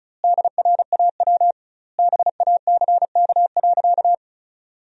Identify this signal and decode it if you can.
Morse code, heard as a keyed tone, about 34 wpm, 700 Hz. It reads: DRAW BACK.
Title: 2x Word-Spacing - Sentences from Top 300 Words - 35wpm